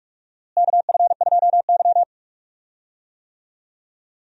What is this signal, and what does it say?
Morse code, heard as a keyed tone, about 45 wpm, 700 Hz. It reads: KF2Y
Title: USA Call Signs - 4 Characters - Repeated 2x - 45wpm